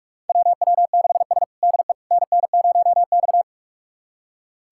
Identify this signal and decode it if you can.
Morse code, heard as a keyed tone, about 46 wpm, 700 Hz. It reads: WW6S DE NN0X
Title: USA Callsign DE Callsign - 4 Characters - 45wpm